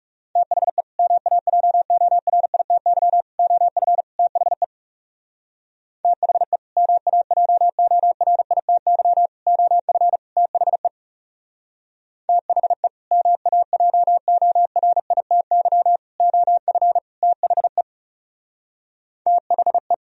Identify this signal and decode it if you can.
Morse code, heard as a keyed tone, about 38 wpm, 700 Hz. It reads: THE MAJORITY OF THE THE MAJORITY OF THE THE MAJORITY OF THE THE